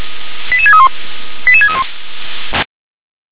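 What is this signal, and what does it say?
Five-tone selective call (ZVEI), heard as quick runs of five tones, tones of about 80 ms. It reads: 88631 88631
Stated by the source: ZVEI five-tone selective call received via SDR.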